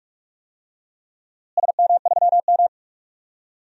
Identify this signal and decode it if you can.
Morse code, heard as a keyed tone, about 46 wpm, 700 Hz. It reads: SM3M
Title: DXCC-100 Call Signs - 4 Characters - Repeated 3x - 45wpm